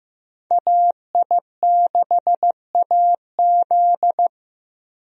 Morse code, heard as a keyed tone, about 15 wpm, 700 Hz. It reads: AI6AZ
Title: USA Call Signs - 5 Characters - Repeated 2x - 15wpm